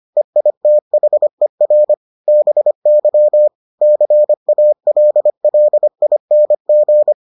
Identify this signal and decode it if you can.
Morse code, heard as a keyed tone, about 25 wpm, 600 Hz. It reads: EITHER BY CALLING